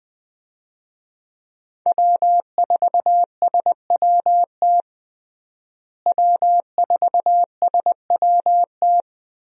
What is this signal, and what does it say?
Morse code, heard as a keyed tone, about 20 wpm, 700 Hz. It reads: W4SWT W4SWT
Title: USA Call Signs - 5 Characters - Repeated 2x - 20wpm